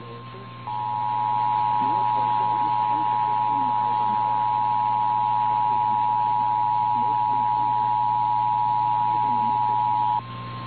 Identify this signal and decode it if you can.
Emergency Alert System (SAME) tones (the 853 and 960 Hz attention tone), with speech. It is heard from 0.6 to 10.2 s.
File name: eas / KNCYKBIE / KNCYKBIE